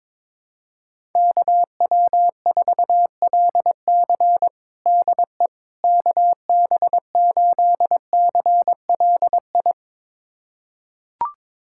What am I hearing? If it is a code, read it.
Morse code, heard as a keyed tone, about 22 wpm, 700 Hz. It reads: KW4LC DE KB8CLI
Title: USA Callsign DE Callsign 22wpm